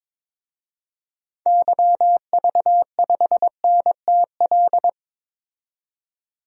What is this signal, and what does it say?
Morse code, heard as a keyed tone, about 22 wpm, 700 Hz. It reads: YV5NTL
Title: DXCC-100 Call Signs - 6 Characters - 22wpm